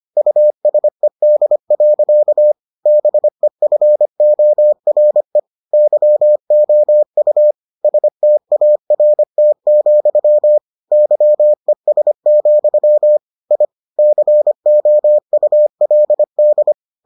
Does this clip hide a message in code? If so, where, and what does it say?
Yes: Morse code from 0.2 to 16.7 s, heard as a keyed tone, about 25 wpm, 600 Hz. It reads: USED. BEFORE YOU START, YES, I COULD